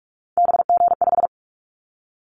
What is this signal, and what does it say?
Morse code, heard as a keyed tone, about 44 wpm, 700 Hz. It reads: BG5
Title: Callsign Prefixes for Top 100 DXCC - 45wpm